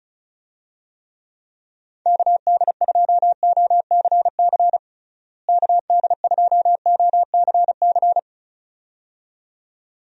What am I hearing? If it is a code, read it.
Morse code, heard as a keyed tone, about 34 wpm, 700 Hz. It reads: KD2OCC KD2OCC
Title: USA Call Signs - 6 Characters - Repeated 2x - 35wpm